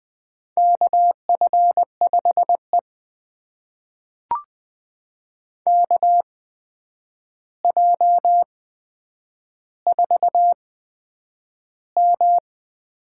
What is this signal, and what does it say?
Morse code, heard as a keyed tone, about 20 wpm, spaced as for 9 wpm, 700 Hz. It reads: KF5E KJ4M
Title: USA Call Signs - 4 Characters - 8x Character Spacing - 20wpm